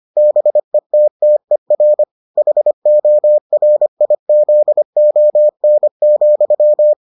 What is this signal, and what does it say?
Morse code, heard as a keyed tone, about 25 wpm, 600 Hz. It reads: BETTER HORIZON,